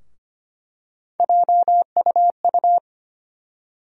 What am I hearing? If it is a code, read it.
Morse code, heard as a keyed tone, about 25 wpm, 700 Hz. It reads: JUU
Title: USA Call Sign Postfixes - 3 Letters 25wpm